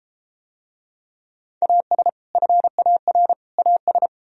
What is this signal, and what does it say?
Morse code, heard as a keyed tone, about 30 wpm, 700 Hz. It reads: AS FAR AS